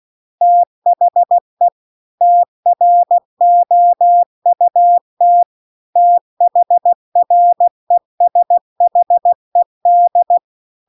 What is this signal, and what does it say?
Morse code, heard as a keyed tone, about 16 wpm, 700 Hz. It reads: THE TROUT THRESHED